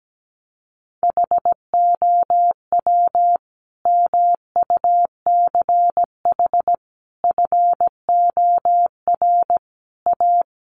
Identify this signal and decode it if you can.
Morse code, heard as a keyed tone, about 17 wpm, 700 Hz. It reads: HOW MUCH FOR A
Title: Sentences from Top 1400 Words 17wpm